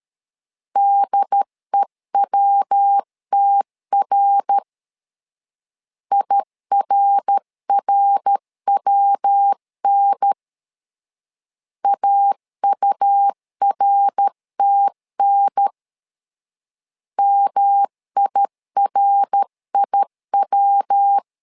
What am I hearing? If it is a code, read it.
Morse code, heard as a keyed tone, about 13 wpm, 800 Hz. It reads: DEWTR IRRWN AURTN MIRIW
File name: LEZIONI / 3 MWRO / 3 MWR 12wpm 1711